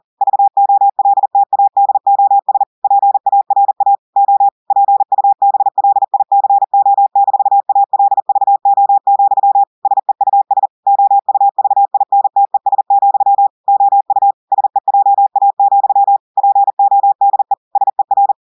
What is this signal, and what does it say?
Morse code, heard as a keyed tone, about 40 wpm, 800 Hz. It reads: VOLTADOS PARA O PUBLICO-ALVO, SEUS OUVINTES, OU SEJA, PODE SER
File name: audios / portugais / portugais texte en clair 2 vitesse 40 mots minutes